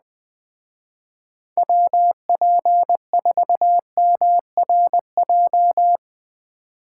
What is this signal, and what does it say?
Morse code, heard as a keyed tone, about 20 wpm, 700 Hz. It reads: WP4MRJ